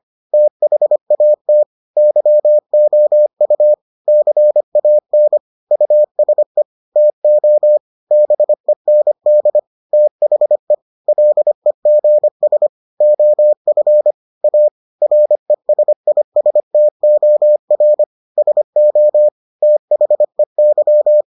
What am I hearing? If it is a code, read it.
Morse code, heard as a keyed tone, about 25 wpm, 600 Hz. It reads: THAT YOU CAN USE TO BEND THE LEGS OF A RESISTOR SO THEY